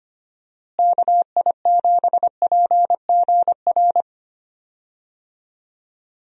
Morse code, heard as a keyed tone, about 25 wpm, 700 Hz. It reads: KI7PGR
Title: USA Call Signs - 6 Characters - Repeated 3x - 25wpm